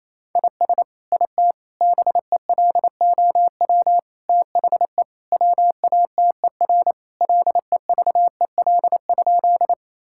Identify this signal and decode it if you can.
Morse code, heard as a keyed tone, about 27 wpm, 700 Hz. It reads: IS IT BELOW THE WATER LEVEL?
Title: Sentences from Top 1000 Words - Review Part 5 - 28wpm